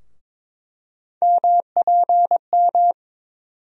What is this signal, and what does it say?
Morse code, heard as a keyed tone, about 22 wpm, 700 Hz. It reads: MPM
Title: USA Call Sign Postfixes - 3 Letters 22wpm